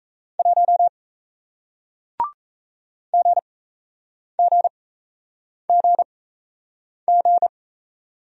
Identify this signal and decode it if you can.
Morse code, heard as a keyed tone, about 35 wpm, spaced as for 8 wpm, 700 Hz. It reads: 1 GGGG
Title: Speed-Racing Single Letter-Number 28wpm